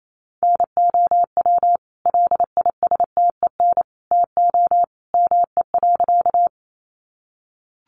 Morse code, heard as a keyed tone, about 28 wpm, 700 Hz. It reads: NOW LISTEN TO ME.
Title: Sentences from Top 400 Words 28wpm